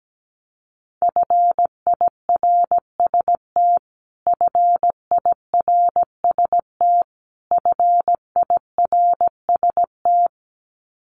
Morse code, heard as a keyed tone, about 17 wpm, 700 Hz. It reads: FIRST FIRST FIRST
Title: Top 300 Words Repeated 17wpm